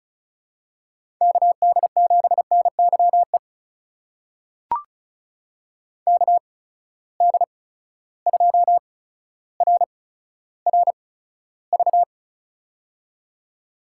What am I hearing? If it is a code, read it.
Morse code, heard as a keyed tone, about 36 wpm, spaced as for 15 wpm, 700 Hz. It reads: KD7NYE KD2RRV
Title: USA Call Signs - 6 Characters - 8x Character Spacing - 35wpm